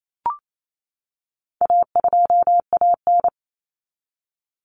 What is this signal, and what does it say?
Morse code, heard as a keyed tone, about 28 wpm, 700 Hz. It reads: A2AN
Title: Common Cut Numbers - 4-Digit 28wpm